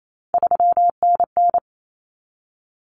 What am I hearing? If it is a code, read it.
Morse code, heard as a keyed tone, about 28 wpm, 700 Hz. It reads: 3NN
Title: Common Cut Numbers - 3-Digit 28wpm